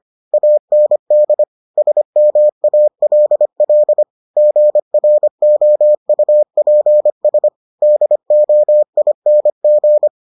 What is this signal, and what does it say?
Morse code, heard as a keyed tone, about 25 wpm, 600 Hz. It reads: AND SMALL GROUPS DOING